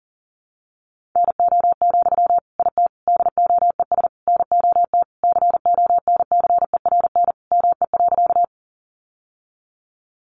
Morse code, heard as a keyed tone, about 38 wpm, 700 Hz. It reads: NO, IT DOES NOT CONCERN ME.
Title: Sentences from Top 1000 Words 40wpm